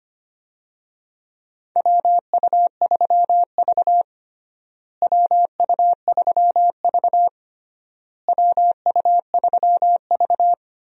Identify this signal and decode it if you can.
Morse code, heard as a keyed tone, about 25 wpm, 700 Hz. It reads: WU3V WU3V WU3V
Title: USA Call Signs - 4 Characters - Repeated 3x - 25wpm